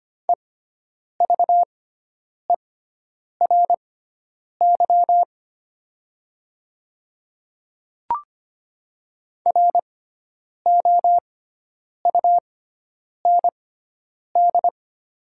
Morse code, heard as a keyed tone, about 25 wpm, spaced as for 9 wpm, 700 Hz. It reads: EVERY ROUND
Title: 6x Character Spacing - Top 300 Words - 25wpm